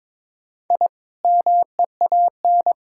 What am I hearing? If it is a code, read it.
Morse code, heard as a keyed tone, about 22 wpm, 700 Hz. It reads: I MEAN
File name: Build A Phrase - 2-Word Phrases - 22wpm